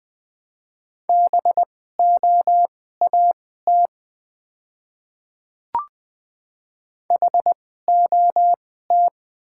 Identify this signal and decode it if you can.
Morse code, heard as a keyed tone, about 21 wpm, spaced as for 15 wpm, 700 Hz. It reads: BOAT HOT
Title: 2x Character Spacing - Top 500 Words - 20wpm